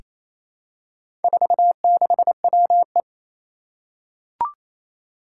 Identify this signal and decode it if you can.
Morse code, heard as a keyed tone, about 27 wpm, 700 Hz. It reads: 46WE
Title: Four Challenging Letter-Number Combinations 28wpm